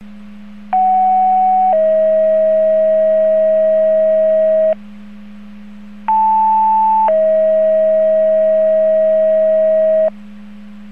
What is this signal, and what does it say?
Two-tone page, heard as a one-second tone, then a three-second tone, about 750 then 640 Hz, about 880 then 640 Hz. It is heard from 0.7 to 10.1 s.